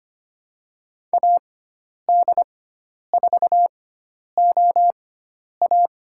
Morse code, heard as a keyed tone, about 25 wpm, spaced as for 10 wpm, 700 Hz. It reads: AD4OA
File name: Lesson 16g - Letter c - Call Signs 25wpm